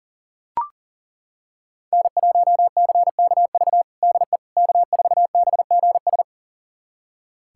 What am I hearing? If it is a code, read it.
Morse code, heard as a keyed tone, about 40 wpm, 700 Hz. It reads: N1CKV DE K4BGS
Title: USA Callsign DE Callsign - 5 Characters - 40wpm